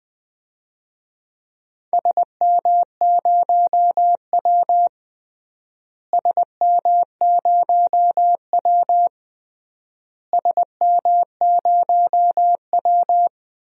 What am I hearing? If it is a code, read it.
Morse code, heard as a keyed tone, about 21 wpm, 700 Hz. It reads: SM0W SM0W SM0W
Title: DXCC-100 Call Signs - 4 Characters - Repeated 3x - 20wpm